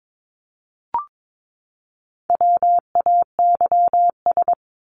Morse code, heard as a keyed tone, about 22 wpm, 700 Hz. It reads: WAYS
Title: ES:S Suffix - Top 100 Words with Base Words 22wpm